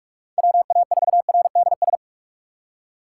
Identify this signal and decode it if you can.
Morse code, heard as a keyed tone, about 42 wpm, 700 Hz. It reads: WA4RDS
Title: USA Call Signs - 6 Characters - 3x Character Spacing - 45wpm